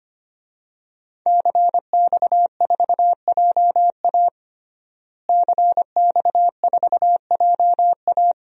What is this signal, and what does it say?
Morse code, heard as a keyed tone, about 25 wpm, 700 Hz. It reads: CX4JA CX4JA
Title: DXCC-100 Call Signs - 5 Characters - Repeated 2x - 25wpm